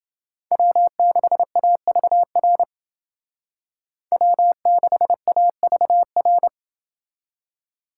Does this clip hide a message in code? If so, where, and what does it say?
Yes: Morse code from 0.5 to 6.5 s, heard as a keyed tone, about 28 wpm, 700 Hz. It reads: W6AVR W6AVR